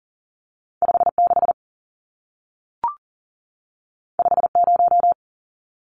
Morse code, heard as a keyed tone, about 40 wpm, 700 Hz. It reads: 56 50
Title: Numbers - 2-Digit 40wpm